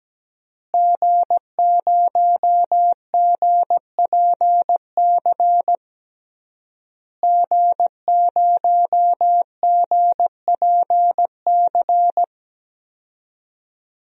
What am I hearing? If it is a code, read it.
Morse code, heard as a keyed tone, about 17 wpm, 700 Hz. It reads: G0GPC G0GPC
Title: DXCC-100 Call Signs - 5 Characters - Repeated 3x - 17wpm